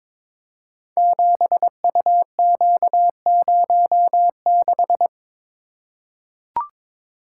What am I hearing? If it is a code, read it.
Morse code, heard as a keyed tone, about 22 wpm, 690 Hz. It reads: 7UQ06
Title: Code Groups - 5 Letter:Numbers - 22wpm